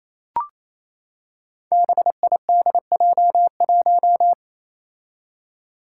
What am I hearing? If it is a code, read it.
Morse code, heard as a keyed tone, about 29 wpm, 700 Hz. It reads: BIDJ1
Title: Five Challenging Letter-Number Combinations 28wpm